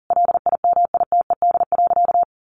Morse code, heard as a keyed tone, about 40 wpm, 700 Hz. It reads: LIMITED.
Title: Sentences from Top 900 Words 40wpm